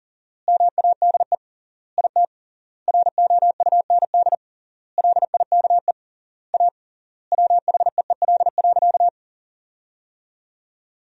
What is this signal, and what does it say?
Morse code, heard as a keyed tone, about 37 wpm, 700 Hz. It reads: MADE IT ROUND LIKE A WHEEL.
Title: 3x Word-Spacing - Sentences from Top 500 Words - 40wpm